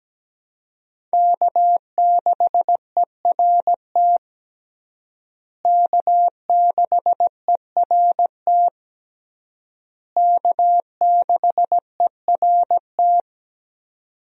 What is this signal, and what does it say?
Morse code, heard as a keyed tone, about 17 wpm, 700 Hz. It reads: K6ERT K6ERT K6ERT
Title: USA Call Signs - 5 Characters - Repeated 3x - 17wpm